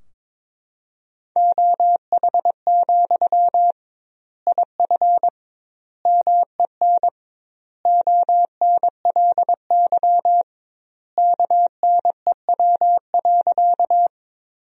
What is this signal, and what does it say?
Morse code, heard as a keyed tone, about 22 wpm, 700 Hz. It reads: OH, IF MEN ONLY KNEW.
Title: 2x Word-Spacing - Sentences from Top 500 Words - 22wpm